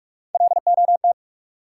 Morse code, heard as a keyed tone, about 43 wpm, 700 Hz. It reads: LOT
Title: Sentences from Top 500 Words - New Word Drill - 45wpm